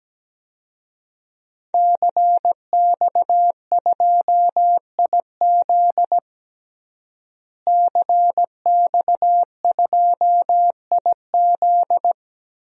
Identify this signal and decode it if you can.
Morse code, heard as a keyed tone, about 17 wpm, 700 Hz. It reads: CX2IZ CX2IZ
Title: DXCC-100 Call Signs - 5 Characters - Repeated 2x - 17wpm